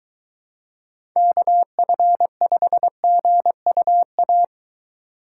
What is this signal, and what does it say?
Morse code, heard as a keyed tone, about 22 wpm, 700 Hz. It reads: KF5GUA